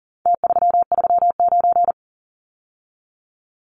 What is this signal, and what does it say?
Morse code, heard as a keyed tone, about 40 wpm, 700 Hz. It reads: T339
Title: Numbers - Contest Style - 4-Digit 40wpm